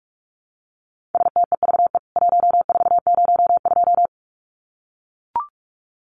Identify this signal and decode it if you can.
Morse code, heard as a keyed tone, about 44 wpm, 700 Hz. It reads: STEVE 1402
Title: CWT Exchange 45wpm